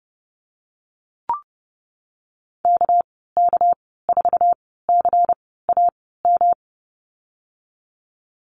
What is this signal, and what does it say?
Morse code, heard as a keyed tone, about 30 wpm, spaced as for 17 wpm, 700 Hz. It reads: KK4CAM
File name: Lesson 37g - Letter x - Call Signs 30wpm